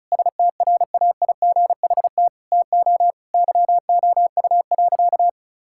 Morse code, heard as a keyed tone, about 36 wpm, 690 Hz. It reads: STRAIGHT TO YOU.
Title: Sentences from Top 1000 Words - Review Part 6 - 35wpm